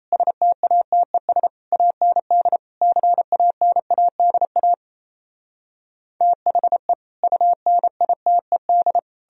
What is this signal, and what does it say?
Morse code, heard as a keyed tone, about 37 wpm, 700 Hz. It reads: STATES AND CANADA THE UNITED